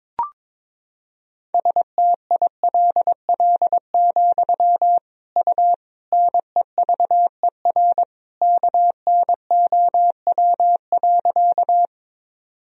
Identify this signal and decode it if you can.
Morse code, heard as a keyed tone, about 22 wpm, 700 Hz. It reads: STILL, U NEVER KNOW.
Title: Sentences from Top 300 Words and Abbreviations 22wpm